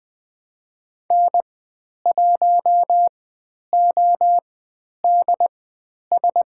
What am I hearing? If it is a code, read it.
Morse code, heard as a keyed tone, about 20 wpm, spaced as for 10 wpm, 700 Hz. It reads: N1ODS
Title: 102 Lesson 17g - Letter u - Call Signs 20wpm